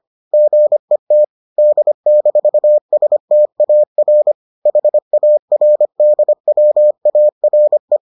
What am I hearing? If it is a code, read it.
Morse code, heard as a keyed tone, about 25 wpm, 600 Hz. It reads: GET D-STAR HARDWARE